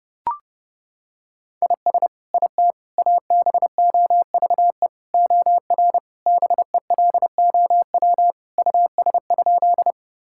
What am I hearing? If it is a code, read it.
Morse code, heard as a keyed tone, about 32 wpm, 700 Hz. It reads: IS IT ABOVE OR BELOW US?
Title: Sentences from Top 1000 Words - Review Part 5 - 30wpm